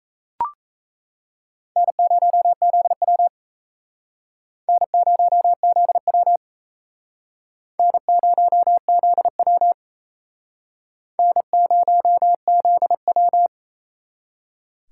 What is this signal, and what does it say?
Morse code, heard as a keyed tone, about 36 wpm, 700 Hz. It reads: N0ZW N0ZW N0ZW N0ZW